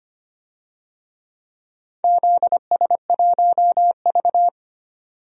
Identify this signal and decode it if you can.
Morse code, heard as a keyed tone, about 25 wpm, 700 Hz. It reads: ZS1V
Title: DXCC-100 Call Signs - 4 Characters - 25wpm